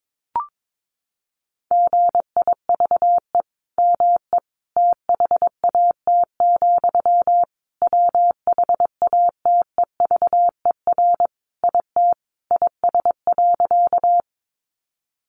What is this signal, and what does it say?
Morse code, heard as a keyed tone, about 22 wpm, 700 Hz. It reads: GIVE ME THAT, WHATEVER IT IS.
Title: Sentences from Top 1100 Words 22wpm